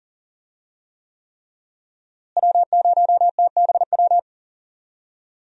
Morse code, heard as a keyed tone, about 40 wpm, 700 Hz. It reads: W0TBW
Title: USA Call Signs - 5 Characters - Repeated 3x - 40wpm